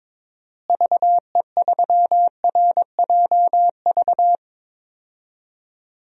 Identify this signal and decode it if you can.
Morse code, heard as a keyed tone, about 22 wpm, 700 Hz. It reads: VE3RJV
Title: DXCC-100 Call Signs - 6 Characters - Repeated 2x - 22wpm